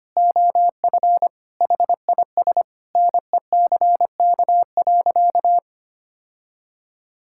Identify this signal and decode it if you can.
Morse code, heard as a keyed tone, about 25 wpm, 700 Hz. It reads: OF HIS NECK.
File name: practice / 25wpm / Sentences with 4-Word Phrases - 25wpm